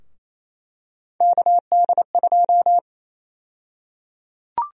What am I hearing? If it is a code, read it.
Morse code, heard as a keyed tone, about 28 wpm, 700 Hz. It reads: KD2
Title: USA Call Sign Prefixes - Repeated 2x - 28wpm